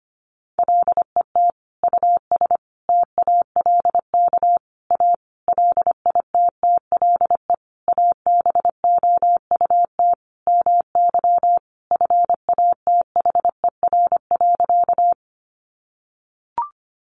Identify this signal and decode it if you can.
Morse code, heard as a keyed tone, about 25 wpm, 700 Hz. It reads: LET US TALK A LITTLE ABOUT MY FATHER.
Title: Sentences from Top 400 Words 25wpm